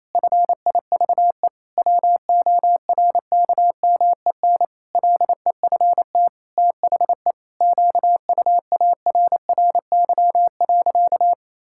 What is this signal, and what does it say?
Morse code, heard as a keyed tone, about 28 wpm, 700 Hz. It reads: FIVE WORKMEN LEFT THE QUARRY.
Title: Pangram Challenge v2 - 28wpm